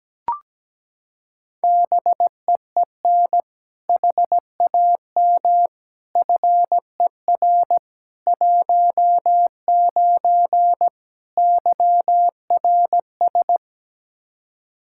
Morse code, heard as a keyed tone, about 17 wpm, 700 Hz. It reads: BEEN HAM FER 19 YRS
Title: Practice QSO - Exchange 3 - More Info - 17wpm